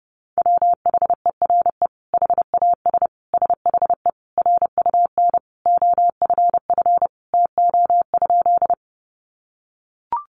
Morse code, heard as a keyed tone, about 30 wpm, 700 Hz. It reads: WHERE HAS SHE RUN OFF TO?
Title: Sentences from Top 300 Words 30wpm